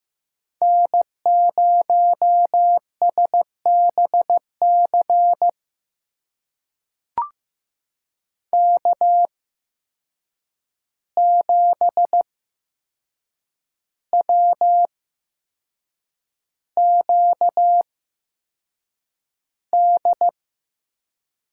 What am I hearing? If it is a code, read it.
Morse code, heard as a keyed tone, about 15 wpm, spaced as for 6 wpm, 700 Hz. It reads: N0SBC K7WQD
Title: USA Call Signs - 8x Character Spacing - 15wpm